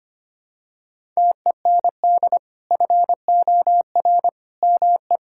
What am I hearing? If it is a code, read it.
Morse code, heard as a keyed tone, about 25 wpm, 700 Hz. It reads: TEND FOR ME
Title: Sentences from Top 1000 Words - Review Part 5 - 25wpm